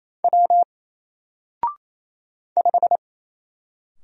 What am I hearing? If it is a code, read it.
Morse code, heard as a keyed tone, about 27 wpm, 700 Hz. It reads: W 5
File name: Single-Letter-Number-28wpm